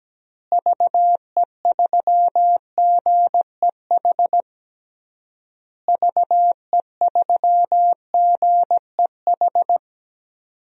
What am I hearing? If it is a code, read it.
Morse code, heard as a keyed tone, about 17 wpm, 700 Hz. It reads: VE3GEH VE3GEH
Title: DXCC-100 Call Signs - 6 Characters - Repeated 3x - 17wpm